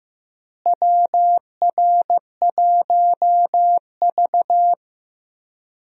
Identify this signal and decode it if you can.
Morse code, heard as a keyed tone, about 15 wpm, 700 Hz. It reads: WR1V
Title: USA Call Signs - 4 Characters - Repeated 2x - 15wpm